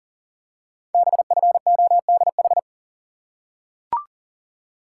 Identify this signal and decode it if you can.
Morse code, heard as a keyed tone, about 38 wpm, 700 Hz. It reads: BFODH
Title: Code Groups - 5 Letters - 40wpm